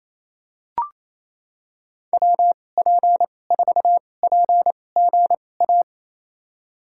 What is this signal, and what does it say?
Morse code, heard as a keyed tone, about 28 wpm, spaced as for 20 wpm, 700 Hz. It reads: WP4PGA
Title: USA Call Signs - 6 Characters - 2x Character Spacing - 28wpm